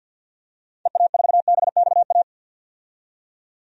Morse code, heard as a keyed tone, about 50 wpm, 700 Hz. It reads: EA4BXA
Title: DXCC-100 Call Signs - 6 Characters - 50wpm